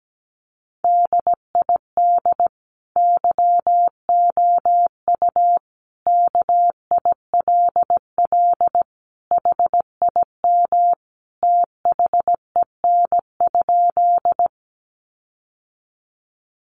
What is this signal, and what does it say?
Morse code, heard as a keyed tone, about 17 wpm, 700 Hz. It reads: DID YOU KILL HIM THEN?
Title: Sentences from Top 1400 Words 17wpm